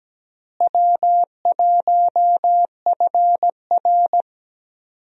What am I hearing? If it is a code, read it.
Morse code, heard as a keyed tone, about 17 wpm, 700 Hz. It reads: W1FR
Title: USA Call Signs - 4 Characters - Repeated 3x - 17wpm